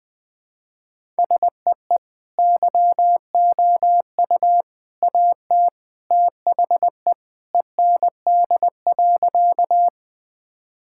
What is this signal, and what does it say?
Morse code, heard as a keyed tone, about 20 wpm, 700 Hz. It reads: SEE YOU AT THE END.
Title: Sentences with 3-Word Phrases - Spoken Beginning - 20wpm